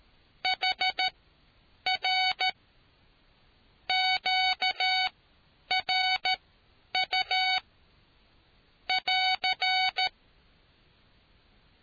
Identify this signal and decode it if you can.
Morse code, heard as a keyed tone, about 13 wpm, spaced as for 7 wpm, 750 Hz. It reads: HR QRU<AR>